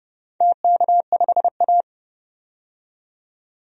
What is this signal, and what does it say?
Morse code, heard as a keyed tone, about 30 wpm, 700 Hz. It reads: TK5A
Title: DXCC-100 Call Signs - 4 Characters - 30wpm